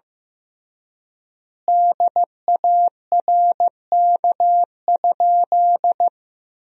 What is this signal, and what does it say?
Morse code, heard as a keyed tone, about 15 wpm, 700 Hz. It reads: DARK?
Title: 3x Word-Spacing - Sentences from Top 500 Words - 15wpm